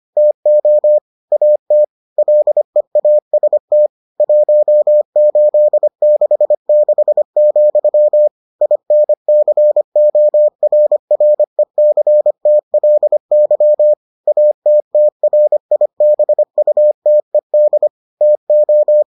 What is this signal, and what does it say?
Morse code, heard as a keyed tone, about 25 wpm, 600 Hz. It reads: TO AT LEAST 1866, INCORRECTLY ATTRIBUTED TO